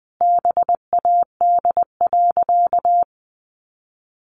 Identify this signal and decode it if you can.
Morse code, heard as a keyed tone, about 20 wpm, 700 Hz. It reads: BAD.
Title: Additional Easy Phrases 20wpm